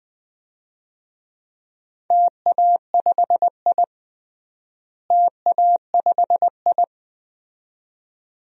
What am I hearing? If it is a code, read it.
Morse code, heard as a keyed tone, about 20 wpm, 700 Hz. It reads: TA5I TA5I
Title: DXCC-100 Call Signs - 4 Characters - Repeated 2x - 20wpm